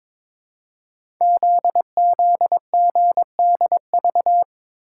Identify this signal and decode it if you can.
Morse code, heard as a keyed tone, about 22 wpm, 700 Hz. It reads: ZZGDV
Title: Code Groups - 5 Letters - 22wpm